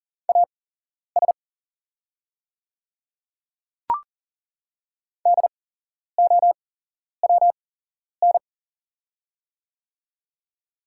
Morse code, heard as a keyed tone, about 40 wpm, spaced as for 11 wpm, 700 Hz. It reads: AS DOWN
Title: 8x Character Spacing - Top 100 Words - 40wpm